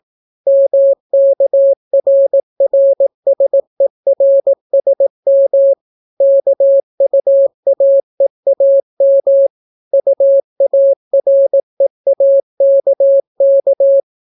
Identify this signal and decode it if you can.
Morse code, heard as a keyed tone, about 18 wpm, 550 Hz. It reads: MKRRSERSM KUAEAM UAREAKK